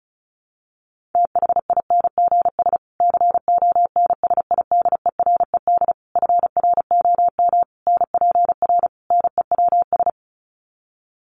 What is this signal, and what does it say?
Morse code, heard as a keyed tone, about 34 wpm, 700 Hz. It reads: THINGS CONSIDERED FROM NPR NEWS
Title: Sets of 5 Words Encore 35wpm